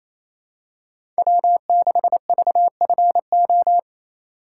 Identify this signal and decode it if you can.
Morse code, heard as a keyed tone, about 28 wpm, 700 Hz. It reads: W6VFO
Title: USA Call Signs - 5 Characters - 4x Character Spacing - 28wpm